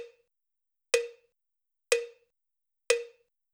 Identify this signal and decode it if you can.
There is no code.